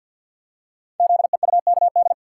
Morse code, heard as a keyed tone, about 53 wpm, 700 Hz. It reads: ZEUKD
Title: Code Groups - 5 Letters - 50wpm